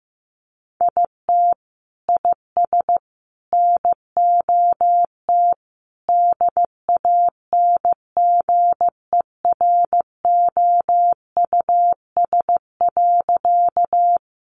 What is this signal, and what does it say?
Morse code, heard as a keyed tone, about 15 wpm, 700 Hz. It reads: IT IS NOT DANGEROUS.
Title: Sentences from Top 1200 Words 15wpm